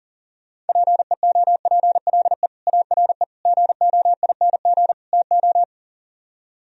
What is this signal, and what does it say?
Morse code, heard as a keyed tone, about 38 wpm, 700 Hz. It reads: PEOPLE ARE GOING TO
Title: Speed-Racing - Sets of 4 Words 40wpm